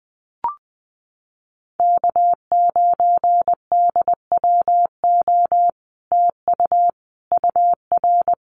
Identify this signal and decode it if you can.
Morse code, heard as a keyed tone, about 20 wpm, 700 Hz. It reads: K9DWO TU UR
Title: Common QSO Phrases Encore 20wpm